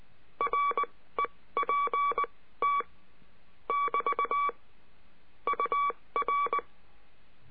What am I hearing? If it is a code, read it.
Morse code, heard as a keyed tone, about 19 wpm, spaced as for 15 wpm, 1140 Hz. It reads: REPT <BT> UR
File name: cursos / CURSO DE TELEGRAFIA / Telegrafia - Trilha 22